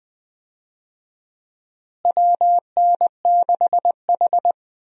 Morse code, heard as a keyed tone, about 20 wpm, 700 Hz. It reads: WN6H
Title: USA Call Signs - 4 Characters - Repeated 3x - 20wpm